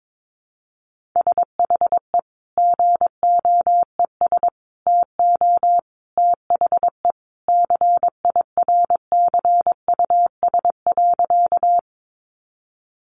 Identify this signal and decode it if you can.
Morse code, heard as a keyed tone, about 22 wpm, 700 Hz. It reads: SHE GOES TO THE CIRCUS.
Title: Easy Phrases 22wpm